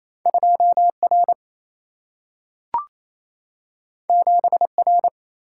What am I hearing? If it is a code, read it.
Morse code, heard as a keyed tone, about 28 wpm, 700 Hz. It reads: 2R 7R
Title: Single-Letter Single-Number 28wpm